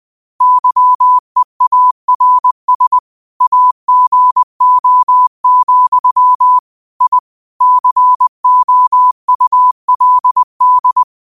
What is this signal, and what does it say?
Morse code, heard as a keyed tone, about 20 wpm, 1000 Hz. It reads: YEARS AGO, I COULD
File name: qotdfiles / 20wpm